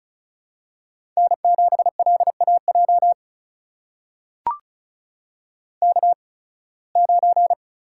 Morse code, heard as a keyed tone, about 36 wpm, spaced as for 23 wpm, 700 Hz. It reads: N7LAJ K9
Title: USA Call Signs - 5 Characters - 8x Character Spacing - 35wpm